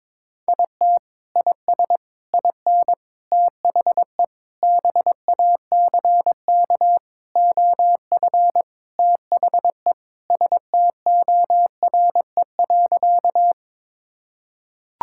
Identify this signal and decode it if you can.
Morse code, heard as a keyed tone, about 21 wpm, 700 Hz. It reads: IT IS IN THE BACK OF THE STORE.
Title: Sentences from Top 1000 Words - Review Part 5 - 22wpm